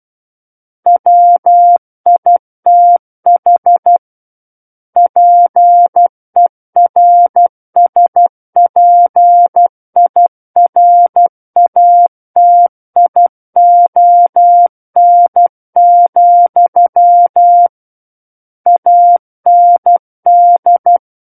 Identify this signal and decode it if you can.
Morse code, heard as a keyed tone, about 12 wpm, 700 Hz. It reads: WITH PERSPIRATION, AND
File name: Book One / 03-Chapter 3 - 12 WPM